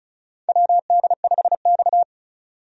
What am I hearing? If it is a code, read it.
Morse code, heard as a keyed tone, about 34 wpm, 700 Hz. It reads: WD5X
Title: USA Call Signs - 4 Characters - 35wpm